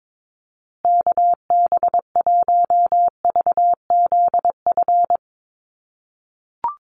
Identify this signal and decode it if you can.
Morse code, heard as a keyed tone, about 22 wpm, 700 Hz. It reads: KB1VZF
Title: USA Call Signs Repeated 2x 22wpm